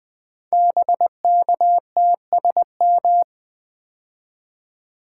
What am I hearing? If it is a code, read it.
Morse code, heard as a keyed tone, about 20 wpm, 700 Hz. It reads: BKTSM
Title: Code Groups - 5 Letters - 20wpm